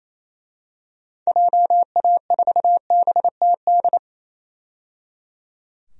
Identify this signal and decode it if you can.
Morse code, heard as a keyed tone, about 28 wpm, 700 Hz. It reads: JA4BTD
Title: DXCC-100 Call Signs - 6 Characters - 28wpm